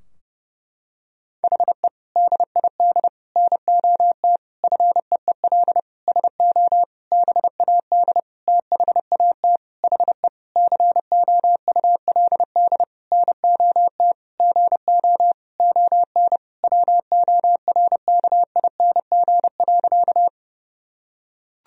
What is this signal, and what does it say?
Morse code, heard as a keyed tone, about 30 wpm, 700 Hz. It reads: HE DID NOT FEEL SO BAD THAT HE COULD NOT GO ON WORKING.
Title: Sentences from Top 1000 Words - Review Part 4 - 30wpm